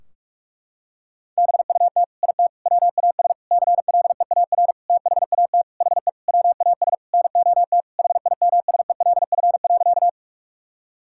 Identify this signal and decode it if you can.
Morse code, heard as a keyed tone, about 41 wpm, 700 Hz. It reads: BUT IT WAS CLEAR THAT HE WAS NOT HIMSELF.
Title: Sentences from Top 1600 Words 45wpm